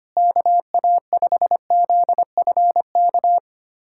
Morse code, heard as a keyed tone, about 25 wpm, 700 Hz. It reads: KA5ZFK
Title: USA Callsign DE Callsign - 6 Characters - 25wpm